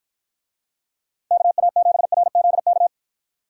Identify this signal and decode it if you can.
Morse code, heard as a keyed tone, about 53 wpm, 700 Hz. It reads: KA7RZK